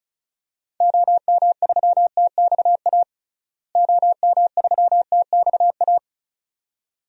Morse code, heard as a keyed tone, about 36 wpm, 700 Hz. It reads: OM3TXA OM3TXA
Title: DXCC-100 Call Signs - 6 Characters - Repeated 2x - 35wpm